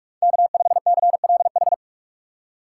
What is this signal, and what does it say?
Morse code, heard as a keyed tone, about 40 wpm, 700 Hz. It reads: K5CLH